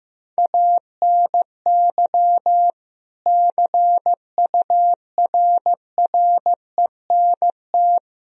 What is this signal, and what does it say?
Morse code, heard as a keyed tone, about 15 wpm, 700 Hz. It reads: ANY CURRENT
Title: Sentences from Top 1000 Words - Review Part 4 - 15wpm